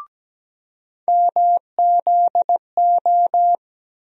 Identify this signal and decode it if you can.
Morse code, heard as a keyed tone, about 17 wpm, 700 Hz. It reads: MZO
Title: USA Call Sign Postfixes - 3 Letters 17wpm